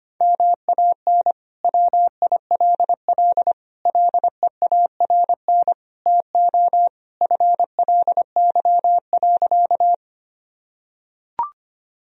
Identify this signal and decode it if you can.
Morse code, heard as a keyed tone, about 25 wpm, 700 Hz. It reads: MAN WILL LEARN TO FLY.
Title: Sentences from Top 1000 Words - Review Part 2 - 25wpm